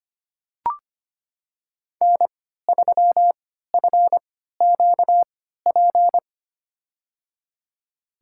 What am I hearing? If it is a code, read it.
Morse code, heard as a keyed tone, about 25 wpm, spaced as for 14 wpm, 700 Hz. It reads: N3FQP
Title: USA Call Signs - 5 Characters - 3x Character Spacing - 25wpm